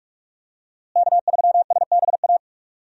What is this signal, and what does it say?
Morse code, heard as a keyed tone, about 48 wpm, 700 Hz. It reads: K3SBA
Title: USA Call Signs - 5 Characters - Repeated 2x - 45wpm